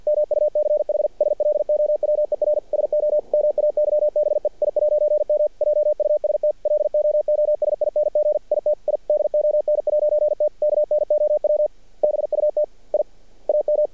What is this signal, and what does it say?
Morse code, heard as a keyed tone, about 50 wpm, 600 Hz. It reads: MUCH SLOWER SO MAYBE I'M JUST LOOSING IT I DON'T KNOW BUT I AM
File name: archived / CRSnet